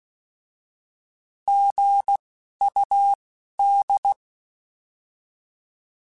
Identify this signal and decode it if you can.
Morse code, heard as a keyed tone, about 16 wpm, spaced as for 11 wpm, 800 Hz. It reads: GUD
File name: Misc / Misc-mp3s / Words @10